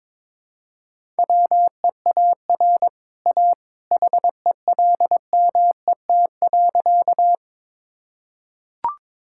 Morse code, heard as a keyed tone, about 22 wpm, 700 Hz. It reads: WEAR A HELMET.
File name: practice / 22wpm / Easy Sentences with Hints - 22wpm